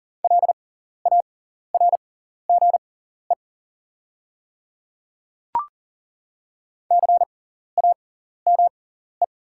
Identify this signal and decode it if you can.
Morse code, heard as a keyed tone, about 39 wpm, spaced as for 14 wpm, 700 Hz. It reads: LARGE CAME
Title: 6x Character Spacing - Top 300 Words - 40wpm